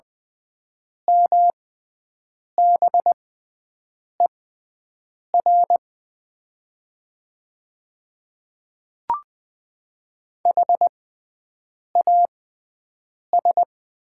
Morse code, heard as a keyed tone, about 20 wpm, spaced as for 7 wpm, 700 Hz. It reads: MBER HAS
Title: 6x Character Spacing - Top 100 Words - 20wpm